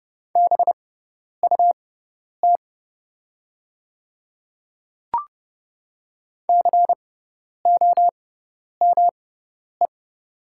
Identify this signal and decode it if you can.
Morse code, heard as a keyed tone, about 30 wpm, spaced as for 10 wpm, 700 Hz. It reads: BUT COME
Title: 6x Character Spacing - Top 100 Words - 30wpm